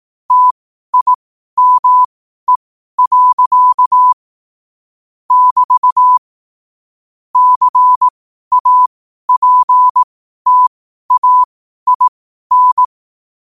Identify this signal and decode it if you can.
Morse code, heard as a keyed tone, about 18 wpm, spaced as for 12 wpm, 1000 Hz. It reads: TIME. <BT> CAPTAIN
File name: qotdfiles / 12wpm